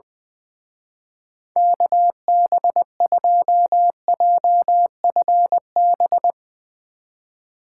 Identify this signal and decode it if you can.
Morse code, heard as a keyed tone, about 20 wpm, 700 Hz. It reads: KB2JFB
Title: Speed Racing - USA 2x3 Call Signs 20wpm